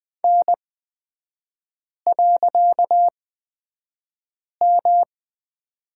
Morse code, heard as a keyed tone, about 20 wpm, spaced as for 5 wpm, 690 Hz. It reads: N.M
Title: 255 Lesson 41e - Period - Three Characters 20wpm